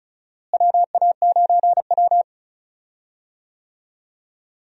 Morse code, heard as a keyed tone, about 36 wpm, 700 Hz. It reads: WA9W
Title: USA Call Signs - 4 Characters - Repeated 3x - 35wpm